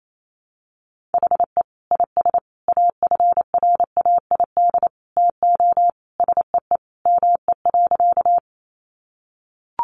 Morse code, heard as a keyed tone, about 28 wpm, 700 Hz. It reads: HE IS AFRAID TO SEE ME.
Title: Sentences from Top 1500 Words 28wpm